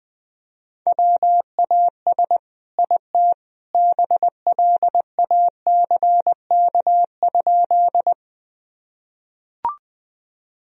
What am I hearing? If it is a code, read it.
Morse code, heard as a keyed tone, about 20 wpm, 700 Hz. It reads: WAS IT BLACK?
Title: Sentences from Top 1000 Words - Review Part 2 - 20wpm